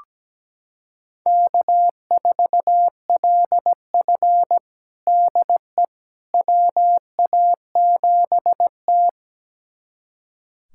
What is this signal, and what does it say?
Morse code, heard as a keyed tone, about 17 wpm, 700 Hz. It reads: K4LF DE WA7T
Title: USA Callsign DE Callsign - 4 Characters - 17wpm